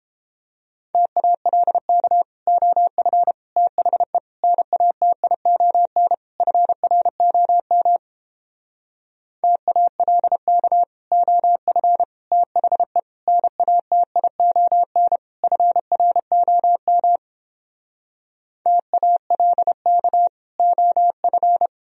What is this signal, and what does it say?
Morse code, heard as a keyed tone, about 29 wpm, 700 Hz. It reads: TALK OF THE NATION FROM TALK OF THE NATION FROM TALK OF